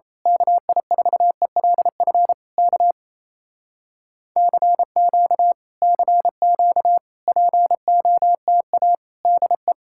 Morse code, heard as a keyed tone, about 26 wpm, 700 Hz. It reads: KI4ELF K CQ CQ POTA DE
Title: Speed-Racing Common QSO Phrases Encore - 28wpm